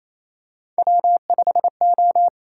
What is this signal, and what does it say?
Morse code, heard as a keyed tone, about 28 wpm, 700 Hz. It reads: W5O
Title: USA Call Signs - 6x Character Spacing - 28wpm